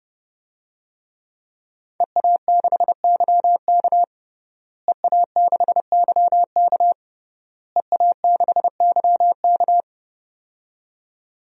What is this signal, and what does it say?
Morse code, heard as a keyed tone, about 30 wpm, 700 Hz. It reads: EA6YK EA6YK EA6YK
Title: DXCC-100 Call Signs - 5 Characters - Repeated 3x - 30wpm